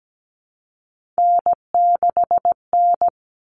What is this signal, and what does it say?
Morse code, heard as a keyed tone, about 17 wpm, 700 Hz. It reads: N6N
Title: Common Cut Numbers - 3-Digit 17wpm